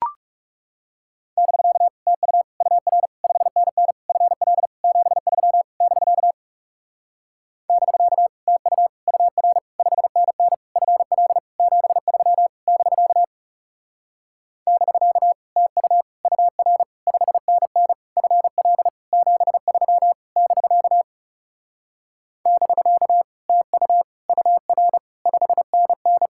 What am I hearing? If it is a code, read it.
Morse code, heard as a keyed tone, about 40 wpm, 700 Hz. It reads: <BK> TU UR 5NN FL 73 <BK> <BK> TU UR 5NN FL 73 <BK> <BK> TU UR 5NN FL 73 <BK> <BK> TU UR 5NN